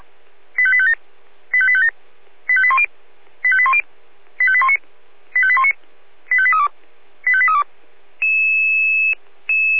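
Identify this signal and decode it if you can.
Five-tone selective call (ZVEI), heard as quick runs of five tones, tones of about 70 ms. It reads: 85857 85857 85810 85810 85819 85819 85832 85832
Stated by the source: ZVEI five-tone selective call 85102, received by SDR.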